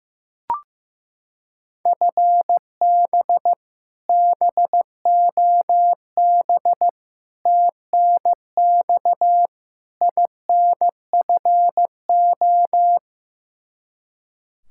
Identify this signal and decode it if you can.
Morse code, heard as a keyed tone, about 15 wpm, 700 Hz. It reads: FB BOB TNX INFO
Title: Practice QSO - Exchange 3 - More Info - 15wpm